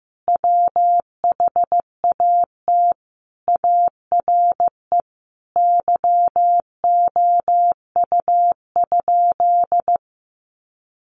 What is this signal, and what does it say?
Morse code, heard as a keyed tone, about 15 wpm, 700 Hz. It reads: WHAT ARE YOU?
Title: Sentences from Top 100 Words 15wpm